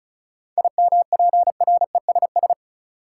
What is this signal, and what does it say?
Morse code, heard as a keyed tone, about 36 wpm, 700 Hz. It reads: IMPRESS
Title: Sentences from Top 1600 Words - New Word Drill - 35wpm